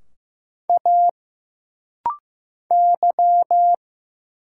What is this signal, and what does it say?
Morse code, heard as a keyed tone, about 15 wpm, 700 Hz. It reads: A Y
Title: Single Letters - Rapid-Fire 15wpm